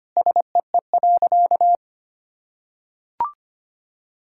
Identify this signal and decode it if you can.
Morse code, heard as a keyed tone, about 25 wpm, 700 Hz. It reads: SEE.
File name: practice / 25wpm / Sentences with 3-Word Phrases - 25wpm